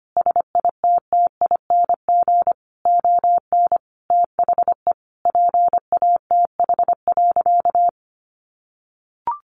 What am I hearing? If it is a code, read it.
Morse code, heard as a keyed tone, about 25 wpm, 700 Hz. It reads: SITTING ON THE PATH.
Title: A Sound Of Thunder 25wpm